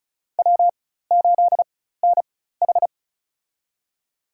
Morse code, heard as a keyed tone, about 36 wpm, spaced as for 16 wpm, 700 Hz. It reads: W8NH
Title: USA Call Signs - 4 Characters - 4x Character Spacing - 35wpm